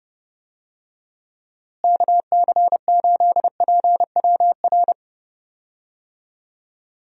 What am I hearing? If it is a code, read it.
Morse code, heard as a keyed tone, about 30 wpm, 700 Hz. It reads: KC8PWR
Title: USA Call Signs - 6 Characters - Repeated 3x - 30wpm